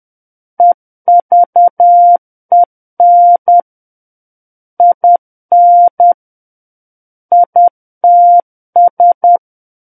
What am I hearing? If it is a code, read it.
Morse code, heard as a keyed tone, about 10 wpm, 700 Hz. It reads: EVEN IN ITS
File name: Book One / 01-Chapter 1 - 10 WPM